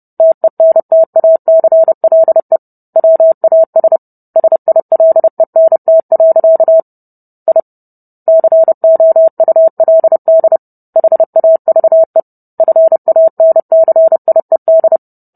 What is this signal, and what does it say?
Morse code, heard as a keyed tone, about 30 wpm, 650 Hz. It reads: TENTACLE WAS SILENT. I COULD HAVE FANCIED